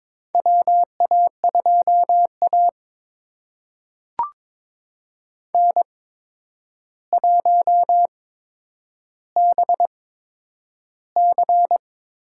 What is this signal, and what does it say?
Morse code, heard as a keyed tone, about 22 wpm, spaced as for 9 wpm, 700 Hz. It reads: WA2A N1BC
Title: USA Call Signs - 4 Characters - 8x Character Spacing - 22wpm